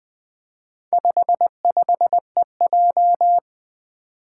Morse code, heard as a keyed tone, about 20 wpm, 700 Hz. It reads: 55EJ
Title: Four Challenging Letter-Number Combinations 20wpm